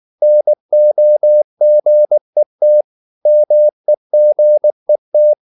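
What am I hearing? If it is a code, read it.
Morse code, heard as a keyed tone, about 19 wpm, 600 Hz. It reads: NOGET MEGET
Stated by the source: Slow (QRS) Morse practice at 18 wpm, CW med Gnister.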